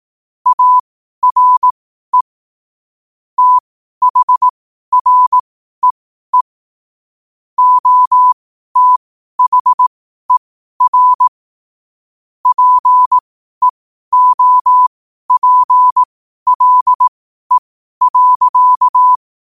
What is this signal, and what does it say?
Morse code, heard as a keyed tone, about 18 wpm, spaced as for 12 wpm, 1000 Hz. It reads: ARE THREE OTHER PEOPLE.
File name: qotdfiles / 12wpm